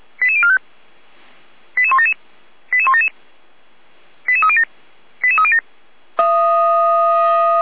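Five-tone selective call (ZVEI), heard as quick runs of five tones, tones of about 70 ms. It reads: 88045 88177 88177 88307 88307
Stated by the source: ZVEI five-tone selective call received with an SDR.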